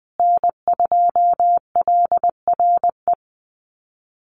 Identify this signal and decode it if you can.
Morse code, heard as a keyed tone, about 20 wpm, 700 Hz. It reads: N2LRE
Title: USA Call Signs Repeated 3x 20wpm